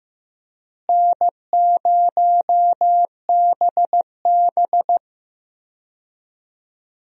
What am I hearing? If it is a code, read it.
Morse code, heard as a keyed tone, about 15 wpm, 700 Hz. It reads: N0BB
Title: USA Call Signs - 4 Characters - Repeated 3x - 15wpm